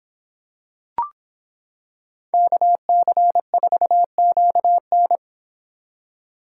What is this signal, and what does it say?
Morse code, heard as a keyed tone, about 26 wpm, 700 Hz. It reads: KC4QN